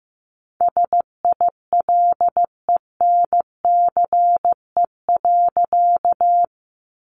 Morse code, heard as a keyed tone, about 15 wpm, 700 Hz. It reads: SILENCE.
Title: A Sound Of Thunder 15wpm